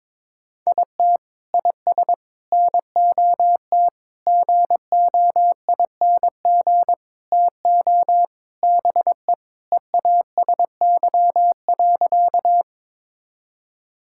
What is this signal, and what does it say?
Morse code, heard as a keyed tone, about 22 wpm, 700 Hz. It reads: IT IS NOT GOING TO BE EASY.
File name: practice / 22wpm / Sentences with 4-Word Phrases - 22wpm